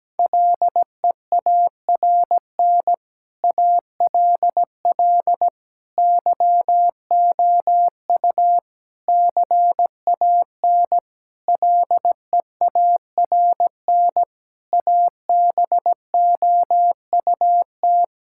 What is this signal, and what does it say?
Morse code, heard as a keyed tone, about 17 wpm, 700 Hz. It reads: LEARN ALL YOU CAN LEARN ABOUT
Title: Sentences from Top 1000 Words - Review Part 5 - 17wpm